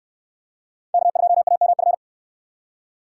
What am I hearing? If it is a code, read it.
Morse code, heard as a keyed tone, about 65 wpm, 700 Hz. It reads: K2AMF